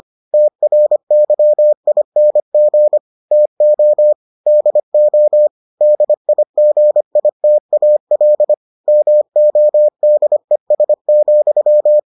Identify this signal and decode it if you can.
Morse code, heard as a keyed tone, about 25 wpm, 600 Hz. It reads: TRYING TO DO DIGITAL MODES,